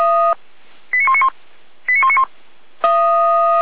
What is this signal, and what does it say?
Five-tone selective call (ZVEI), heard as quick runs of five tones, tones of about 70 ms. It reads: 89191 89191